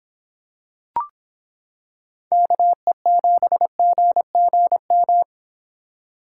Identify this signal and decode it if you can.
Morse code, heard as a keyed tone, about 26 wpm, 700 Hz. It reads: KE7GGM